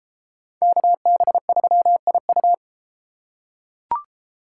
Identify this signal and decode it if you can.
Morse code, heard as a keyed tone, about 31 wpm, 700 Hz. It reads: KB3IU